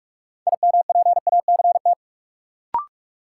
Morse code, heard as a keyed tone, about 46 wpm, 700 Hz. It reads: IMPACT
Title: Sentences from Top 1000 Words - Review Part 6 - New Word Drill - 45wpm